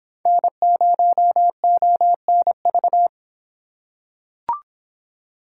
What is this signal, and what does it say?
Morse code, heard as a keyed tone, about 26 wpm, 700 Hz. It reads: N0ONV